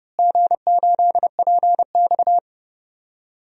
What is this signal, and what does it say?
Morse code, heard as a keyed tone, about 30 wpm, 700 Hz. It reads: G8PX
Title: DXCC-100 Call Signs - 4 Characters - Repeated 3x - 30wpm